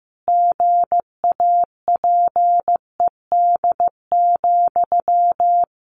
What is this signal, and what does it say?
Morse code, heard as a keyed tone, about 15 wpm, 700 Hz. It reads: GAPED,
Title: A Sound Of Thunder 15wpm